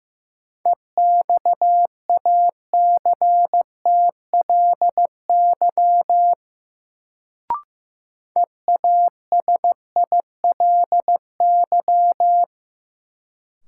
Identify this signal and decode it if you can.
Morse code, heard as a keyed tone, about 15 wpm, 700 Hz. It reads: EXACTLY EASILY
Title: Sentences from Top 900 Words - New Word Drill - 15wpm